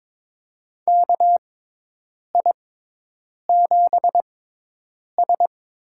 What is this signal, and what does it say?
Morse code, heard as a keyed tone, about 22 wpm, spaced as for 8 wpm, 700 Hz. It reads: KI7S
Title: USA Call Signs - 6x Character Spacing - 22wpm